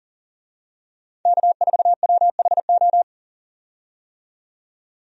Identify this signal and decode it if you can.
Morse code, heard as a keyed tone, about 40 wpm, 700 Hz. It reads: K4WHO
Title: USA Call Signs - 5 Characters - 40wpm